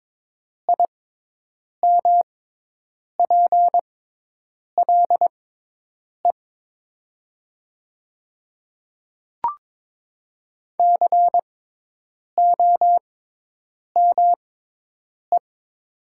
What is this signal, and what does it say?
Morse code, heard as a keyed tone, about 22 wpm, spaced as for 8 wpm, 700 Hz. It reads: IMPLE COME
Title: 6x Character Spacing - Top 500 Words - 22wpm